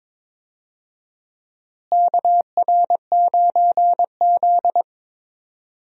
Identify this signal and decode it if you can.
Morse code, heard as a keyed tone, about 21 wpm, 700 Hz. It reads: KR9Z
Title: USA Call Signs - 4 Characters - Repeated 3x - 22wpm